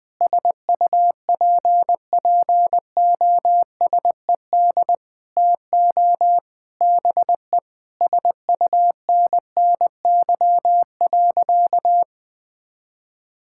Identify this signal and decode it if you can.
Morse code, heard as a keyed tone, about 20 wpm, 700 Hz. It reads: SUPPOSED TO BE SUNNY.
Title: Sentences with 3-Word Phrases - Spoken Beginning - 20wpm